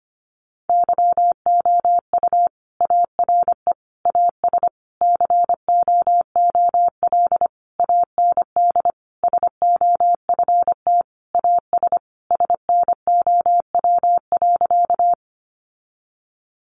Morse code, heard as a keyed tone, about 25 wpm, 700 Hz. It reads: YOU ARE AS COOL AND SOFT AS SNOW.
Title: Sentences from Top 1500 Words 25wpm